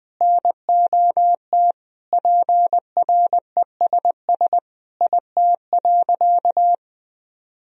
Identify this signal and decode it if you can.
Morse code, heard as a keyed tone, about 21 wpm, 700 Hz. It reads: NOT PRESS IT.
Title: Sentences from Top 1000 Words - Review Part 1 - 20wpm